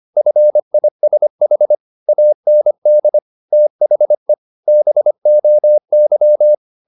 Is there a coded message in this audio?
Yes: Morse code from 0.2 to 6.5 s, heard as a keyed tone, about 25 wpm, 600 Hz. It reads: FISH AND THE BOY